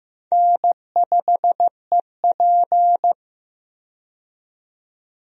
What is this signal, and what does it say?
Morse code, heard as a keyed tone, about 15 wpm, 700 Hz. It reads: N5EP
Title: USA Call Signs - 4 Characters - 15wpm